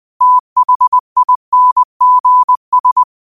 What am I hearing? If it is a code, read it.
Morse code, heard as a keyed tone, about 20 wpm, 1000 Hz. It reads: THINGS
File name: qotdfiles / 20wpm